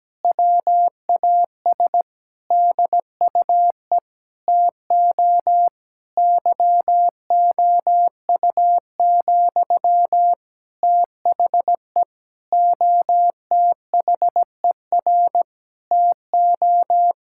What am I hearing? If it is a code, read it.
Morse code, heard as a keyed tone, about 17 wpm, 700 Hz. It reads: WAS DUE TO YOU, THE OTHER TO
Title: Sentences from Top 1000 Words - Review Part 6 - 17wpm